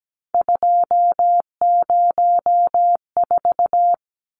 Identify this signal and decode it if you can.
Morse code, heard as a keyed tone, about 17 wpm, 700 Hz. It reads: 204
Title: Numbers - 3-Digit 17wpm